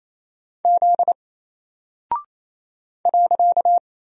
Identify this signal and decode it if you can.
Morse code, heard as a keyed tone, about 27 wpm, 700 Hz. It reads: Z .
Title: Single Character (Letter:Number:Punctuation:Prosign) 28wpm